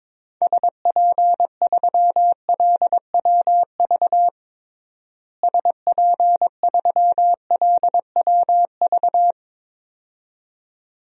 Morse code, heard as a keyed tone, about 22 wpm, 700 Hz. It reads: SP3LWV SP3LWV
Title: DXCC-100 Call Signs - 6 Characters - Repeated 2x - 22wpm